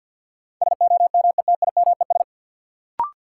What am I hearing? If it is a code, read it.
Morse code, heard as a keyed tone, about 52 wpm, 700 Hz. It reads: SOMETIMES
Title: Sentences from Top 600 Words - New Word Drill - 50wpm